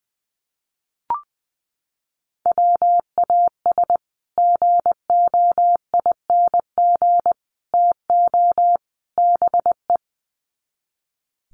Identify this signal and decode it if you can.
Morse code, heard as a keyed tone, about 20 wpm, 700 Hz. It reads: WAS GOING TO BE
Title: Sets of 4 Words 20wpm